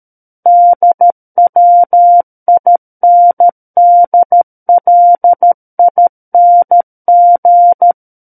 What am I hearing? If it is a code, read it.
Morse code, heard as a keyed tone, about 13 wpm, 700 Hz. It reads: DWINDLING
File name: Book One / 04-Chapter 4 - 13 WPM